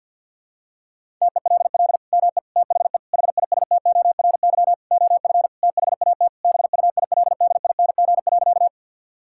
Morse code, heard as a keyed tone, about 52 wpm, 700 Hz. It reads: TELL ME THE HISTORY OF THAT BUILDING.
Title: Sentences from Top 1000 Words - Review Part 5 - 50wpm